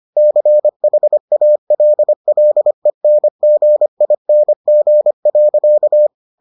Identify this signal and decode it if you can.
Morse code, heard as a keyed tone, about 25 wpm, 600 Hz. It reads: CHALLENGING.